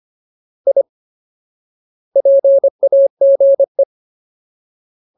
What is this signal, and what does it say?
Morse code, heard as a keyed tone, about 25 wpm, 550 Hz. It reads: I PAGE
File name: Ultimate Word List / 25WPM Ultimate